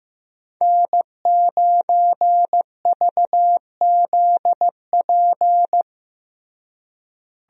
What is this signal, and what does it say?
Morse code, heard as a keyed tone, about 15 wpm, 700 Hz. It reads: N9VZP
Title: USA Call Signs - 5 Characters - Repeated 2x - 15wpm